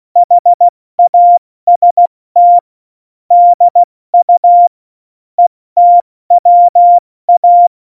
Morse code, heard as a keyed tone, about 16 wpm, 700 Hz. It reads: HAST DU ETWA
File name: Morsekurs / Witze / Witz 30. April 2019, 70 F80